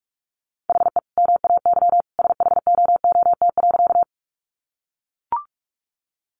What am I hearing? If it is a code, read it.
Morse code, heard as a keyed tone, about 44 wpm, 700 Hz. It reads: HE MAY SHOOT.
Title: Sentences from Top 1200 Words 45wpm